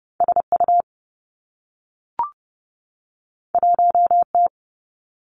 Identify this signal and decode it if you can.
Morse code, heard as a keyed tone, about 30 wpm, 700 Hz. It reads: S U 1 T
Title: Two Challenging Letter-Number Combinations 30wpm